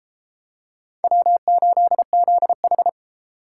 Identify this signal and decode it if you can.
Morse code, heard as a keyed tone, about 33 wpm, 700 Hz. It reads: W8ZH